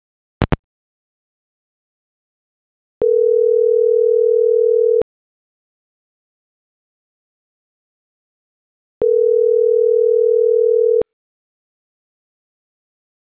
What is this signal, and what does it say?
Ringback tone (North America), 440 + 480 Hz.